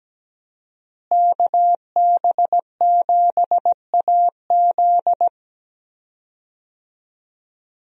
Morse code, heard as a keyed tone, about 17 wpm, 700 Hz. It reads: KB7AZ
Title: USA Call Signs - 5 Characters - Repeated 3x - 17wpm